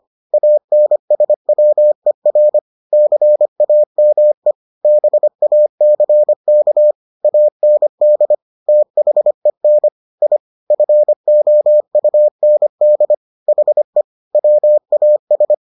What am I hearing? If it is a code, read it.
Morse code, heard as a keyed tone, about 25 wpm, 600 Hz. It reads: ANSWER CAME BACK AND THEN I FOUND HE WAS